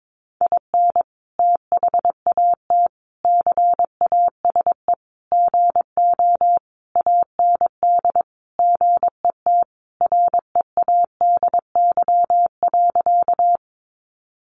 Morse code, heard as a keyed tone, about 22 wpm, 700 Hz. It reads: IN THAT CASE GO AND GET READY.
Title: Sentences from Top 700 Words 22wpm